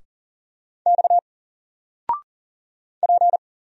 Morse code, heard as a keyed tone, about 40 wpm, 700 Hz. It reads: X P
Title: Single Letter-Number 40wpm